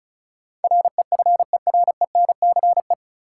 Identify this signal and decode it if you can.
Morse code, heard as a keyed tone, about 34 wpm, 700 Hz. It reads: REFERENCE
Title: Sentences from Top 1200 Words - New Word Drill - 35wpm